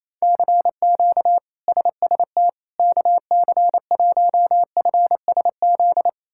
Morse code, heard as a keyed tone, about 28 wpm, 700 Hz. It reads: CQ SST KC1FSZ
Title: K1USN SST QSO Practice - 28wpm